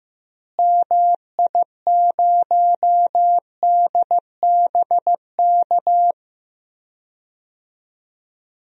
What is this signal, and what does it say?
Morse code, heard as a keyed tone, about 15 wpm, 700 Hz. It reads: MI0DBK
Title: DXCC-100 Call Signs - 6 Characters - Repeated 2x - 15wpm